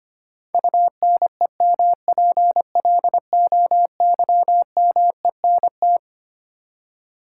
Morse code, heard as a keyed tone, about 25 wpm, 700 Hz. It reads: UNEMPLOYMENT
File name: Twelve Letter Words - 25wpm